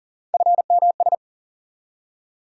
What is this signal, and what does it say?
Morse code, heard as a keyed tone, about 40 wpm, 700 Hz. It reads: FMS
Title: USA Call Sign Postfixes - 3 Letters 40wpm